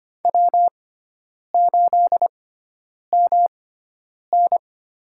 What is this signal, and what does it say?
Morse code, heard as a keyed tone, about 25 wpm, spaced as for 9 wpm, 700 Hz. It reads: W8MN
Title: USA Call Signs - 4 Characters - 6x Character Spacing - 25wpm